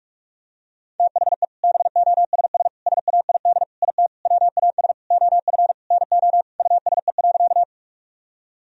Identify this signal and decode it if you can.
Morse code, heard as a keyed tone, about 47 wpm, 700 Hz. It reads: THE BOSS SAID IT WAS OF NO USE.
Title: Sentences from Top 1000 Words - Review Part 4 - 45wpm